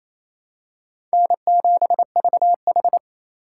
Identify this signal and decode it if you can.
Morse code, heard as a keyed tone, about 28 wpm, 700 Hz. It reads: N7VH
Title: USA Call Signs - 4 Characters - 2x Character Spacing - 28wpm